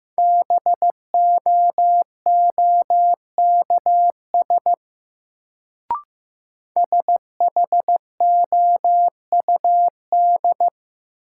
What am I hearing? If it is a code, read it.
Morse code, heard as a keyed tone, about 15 wpm, 700 Hz. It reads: BOOKS SHOUD
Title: Sentences from Top 1000 Words - Review Part 4 - New Word Drill - 15wpm